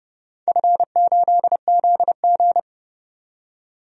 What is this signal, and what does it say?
Morse code, heard as a keyed tone, about 29 wpm, 700 Hz. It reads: F8ZG
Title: DXCC-100 Call Signs - 4 Characters - 30wpm